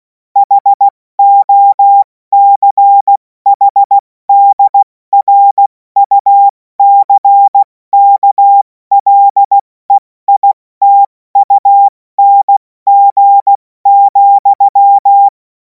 Morse code, heard as a keyed tone, about 16 wpm, spaced as for 14 wpm, 800 Hz. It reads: HOCHDRUCKLEITUNG,
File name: Morsekurs / Witze / Witz 17. Februar 2026 70 F80